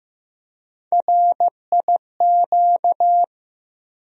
Morse code, heard as a keyed tone, about 15 wpm, 700 Hz. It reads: RIQ
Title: USA Call Sign Postfixes - 3 Letters 15wpm